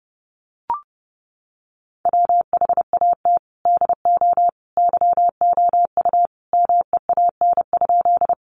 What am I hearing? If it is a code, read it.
Morse code, heard as a keyed tone, about 30 wpm, 700 Hz. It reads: WHAT DO YOU MEAN?
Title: Sentences from Top 200 Words 30wpm